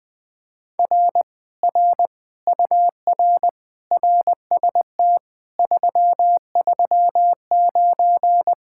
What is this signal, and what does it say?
Morse code, heard as a keyed tone, about 21 wpm, 700 Hz. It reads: R R UR RST 339
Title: Speed-Racing Common QSO Phrases - 20wpm